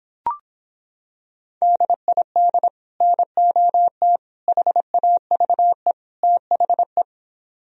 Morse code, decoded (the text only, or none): DID NOT HAVE THE